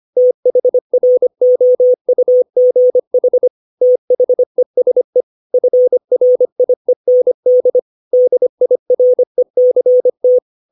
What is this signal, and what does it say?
Morse code, heard as a keyed tone, about 25 wpm, 500 Hz. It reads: THROUGH THESE FRIEND DIRECT